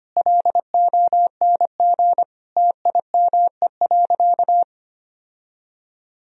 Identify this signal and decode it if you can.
Morse code, heard as a keyed tone, about 25 wpm, 700 Hz. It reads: LONG TIME.
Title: Sentences from Top 1000 Words - Review Part 2 - 25wpm